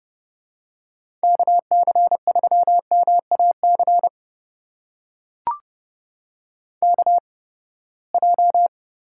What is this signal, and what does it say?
Morse code, heard as a keyed tone, about 30 wpm, spaced as for 21 wpm, 700 Hz. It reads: KC3MAC KJ
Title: USA Call Signs - 6 Characters - 8x Character Spacing - 30wpm